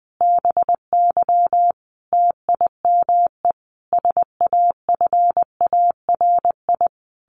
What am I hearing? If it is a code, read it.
Morse code, heard as a keyed tone, about 20 wpm, 700 Hz. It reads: BY TIME SAFARI
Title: A Sound Of Thunder 20wpm